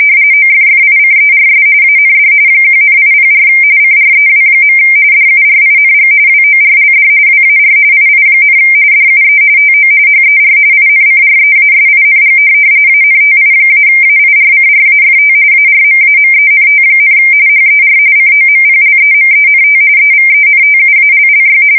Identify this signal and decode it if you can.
RTTY, about 45.45 baud. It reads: OFFICERS AND 1,804 CREW. DURING WORLD WAR II, DUE TO HER INCREASING ANTI-AIRCRAFT ARMAMENT, HER COMPLEMENT EVENTUALLY
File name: fsk170 / USS-New-Jersey